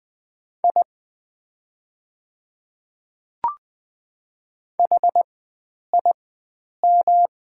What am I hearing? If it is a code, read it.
Morse code, heard as a keyed tone, about 20 wpm, spaced as for 9 wpm, 700 Hz. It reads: I HIM
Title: 4x Character Spacing - Top 100 Words - 20wpm